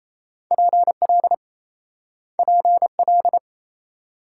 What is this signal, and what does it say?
Morse code, heard as a keyed tone, about 36 wpm, 700 Hz. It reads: PL PL